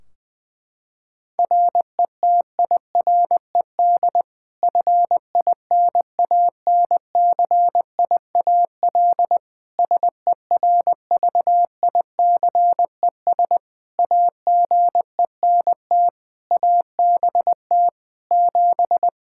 Morse code, heard as a keyed tone, about 20 wpm, 700 Hz. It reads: RETIRED FINANCIAL SERVICES AGENT ABT 7
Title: K1USN SST QSO Practice 20wpm